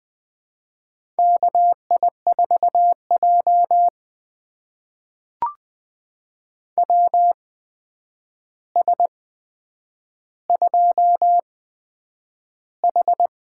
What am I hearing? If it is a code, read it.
Morse code, heard as a keyed tone, about 20 wpm, spaced as for 9 wpm, 700 Hz. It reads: KI4J WS2H
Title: USA Call Signs - 4 Characters - 8x Character Spacing - 20wpm